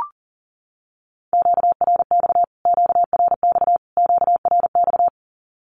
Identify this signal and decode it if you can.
Morse code, heard as a keyed tone, about 40 wpm, 700 Hz. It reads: QRX QRX QRX
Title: Q-Codes Repeated 3x 40wpm